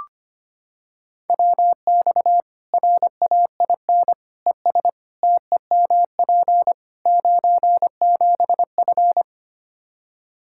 Morse code, heard as a keyed tone, about 25 wpm, 700 Hz. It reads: WX RAIN ES TEMP 97F
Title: Practice QSO - Exchange 2 - Info - 25wpm